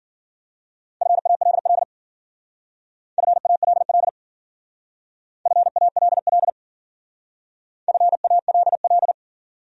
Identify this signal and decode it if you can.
Morse code, heard as a keyed tone, about 50 wpm, 700 Hz. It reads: FALL FALL FALL FALL